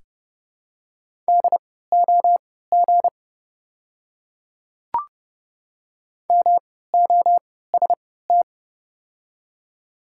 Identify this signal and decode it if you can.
Morse code, heard as a keyed tone, about 31 wpm, spaced as for 17 wpm, 700 Hz. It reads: DOG MOST
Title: 3x Character Spacing - Top 500 Words - 30wpm